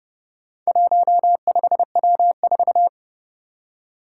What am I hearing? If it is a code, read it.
Morse code, heard as a keyed tone, about 29 wpm, 700 Hz. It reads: 15W4
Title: Four Challenging Letter-Number Combinations 30wpm